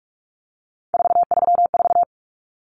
Morse code, heard as a keyed tone, about 44 wpm, 700 Hz. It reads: 434
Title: Common Cut Numbers - 3-Digit 45wpm